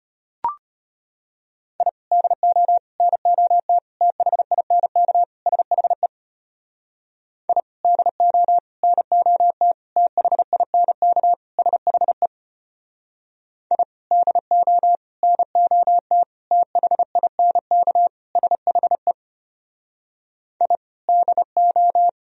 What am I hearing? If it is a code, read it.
Morse code, heard as a keyed tone, about 33 wpm, 700 Hz. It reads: I DO NOT THINK SHE I DO NOT THINK SHE I DO NOT THINK SHE I DO